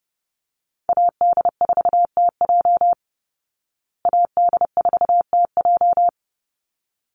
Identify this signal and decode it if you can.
Morse code, heard as a keyed tone, about 30 wpm, 700 Hz. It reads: AD4TJ AD4TJ
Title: Call Signs Repeated 3x - 30wpm